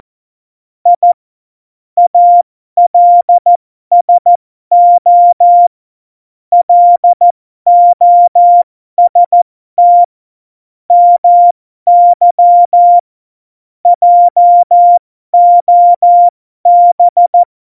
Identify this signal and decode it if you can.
Morse code, heard as a keyed tone, about 14 wpm, spaced as for 12 wpm, 700 Hz. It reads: I ALSO LOST MY JOB
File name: Morsekurs / Witze